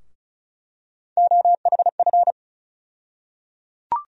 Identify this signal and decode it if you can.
Morse code, heard as a keyed tone, about 34 wpm, 700 Hz. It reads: OHF
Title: USA Call Sign Postfixes - 3 Letters 35wpm